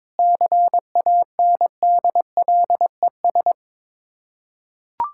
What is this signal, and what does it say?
Morse code, heard as a keyed tone, about 22 wpm, 700 Hz. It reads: CANDLES
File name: Difficult Words with Hints - 22wpm